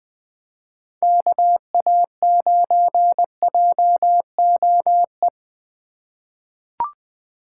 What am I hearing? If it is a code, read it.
Morse code, heard as a keyed tone, about 20 wpm, 700 Hz. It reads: KA9JOE
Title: USA Call Signs - 6 Characters - 2x Character Spacing - 20wpm